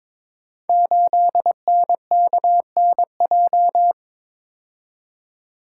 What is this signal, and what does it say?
Morse code, heard as a keyed tone, about 22 wpm, 690 Hz. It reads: 8NKNJ
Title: Code Groups - 5 Letter:Numbers - 22wpm